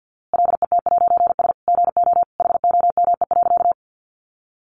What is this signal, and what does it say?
Morse code, heard as a keyed tone, about 50 wpm, 700 Hz. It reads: LET'S GO HOME.
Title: A Sound Of Thunder 50wpm